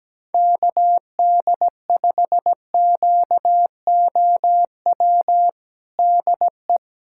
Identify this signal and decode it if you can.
Morse code, heard as a keyed tone, about 17 wpm, 700 Hz. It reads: KD5QOW DE
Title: USA Callsign DE Callsign 17wpm